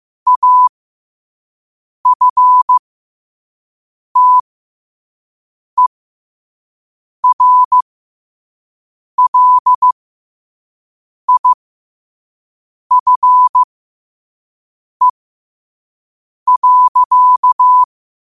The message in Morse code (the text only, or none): AFTERLIFE.